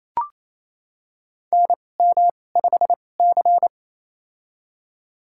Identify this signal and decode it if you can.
Morse code, heard as a keyed tone, about 28 wpm, spaced as for 20 wpm, 700 Hz. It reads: NM5C
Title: USA Call Signs - 2x Character Spacing - 28wpm